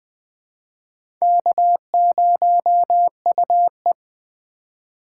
Morse code, heard as a keyed tone, about 20 wpm, 700 Hz. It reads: K0UE
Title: USA Call Signs - 4 Characters - 4x Character Spacing - 20wpm